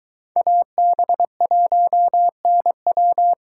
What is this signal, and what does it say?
Morse code, heard as a keyed tone, about 23 wpm, 700 Hz. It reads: AB1NW